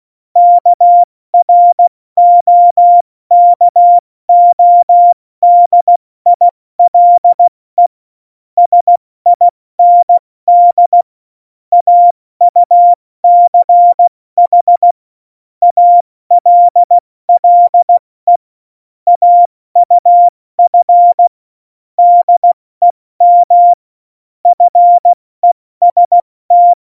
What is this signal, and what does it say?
Morse code, heard as a keyed tone, about 16 wpm, spaced as for 14 wpm, 700 Hz. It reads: KROKODILE SIND AUCH ALLE AUF DEM FEST
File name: Morsekurs / Witze / Witz 30. Oktober 2018, 70 F80